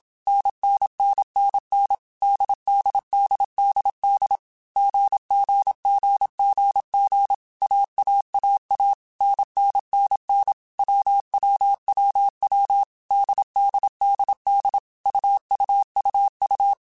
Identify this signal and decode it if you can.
Morse code, heard as a keyed tone, about 27 wpm, 800 Hz. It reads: NNNNN DDDDD GGGGG AAAA NNNN WWWW DDDD UUUU
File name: Basic Sets / Basic Letter Groups Collective7 / AWUNDG@25